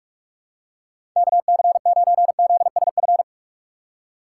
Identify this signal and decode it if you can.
Morse code, heard as a keyed tone, about 43 wpm, 700 Hz. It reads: KC9ZSF
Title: USA Call Signs - 6 Characters - 6x Character Spacing - 45wpm